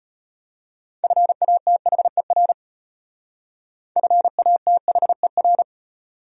Morse code, heard as a keyed tone, about 38 wpm, 700 Hz. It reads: FATHER FATHER